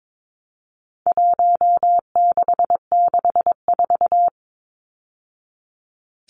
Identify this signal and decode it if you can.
Morse code, heard as a keyed tone, about 22 wpm, 700 Hz. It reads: 1664
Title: Numbers - Contest Style - 4-Digit 22wpm